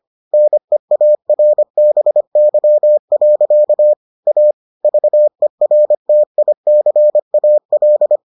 Morse code, heard as a keyed tone, about 25 wpm, 600 Hz. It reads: NEARBY. A VERTICAL